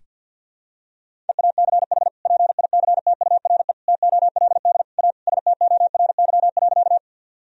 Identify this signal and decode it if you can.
Morse code, heard as a keyed tone, about 46 wpm, 700 Hz. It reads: EACH PICTURE TOLD A STORY.
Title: Sentences from Top 1000 Words - Review Part 1 - 50wpm